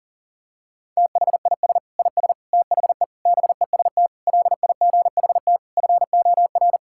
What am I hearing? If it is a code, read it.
Morse code, heard as a keyed tone, about 40 wpm, 700 Hz. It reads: THIS IS THE BEST LIGHT FOR
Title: Sentences from Top 1000 Words - Review Part 6 - 40wpm